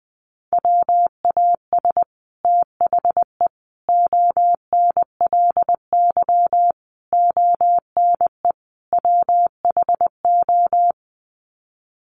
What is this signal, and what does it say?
Morse code, heard as a keyed tone, about 20 wpm, 700 Hz. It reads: WAS THE ONLY ONE WHO
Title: Sets of 5 Words Encore2 20wpm